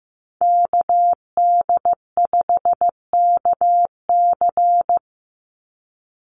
Morse code, heard as a keyed tone, about 15 wpm, 700 Hz. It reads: KD5KC
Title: NA SOTA Activator Callsigns 15wpm